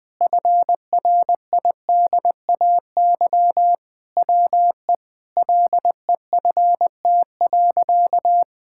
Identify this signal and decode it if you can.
Morse code, heard as a keyed tone, about 20 wpm, 700 Hz. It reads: FRIDAY WE LEFT.
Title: Sentences with 6-Word Phrases - Spoken Beginning - 20wpm